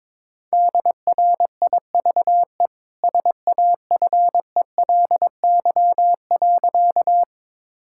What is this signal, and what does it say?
Morse code, heard as a keyed tone, about 22 wpm, 690 Hz. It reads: DRIVE SAFELY.
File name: practice / 22wpm / Easy Sentences with Hints - 22wpm